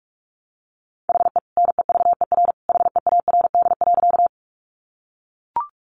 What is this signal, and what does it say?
Morse code, heard as a keyed tone, about 44 wpm, 700 Hz. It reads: HE NEVER HEARD.
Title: Sentences from Top 400 Words 45wpm